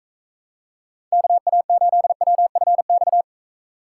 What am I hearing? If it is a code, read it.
Morse code, heard as a keyed tone, about 45 wpm, 700 Hz. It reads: KA8WFX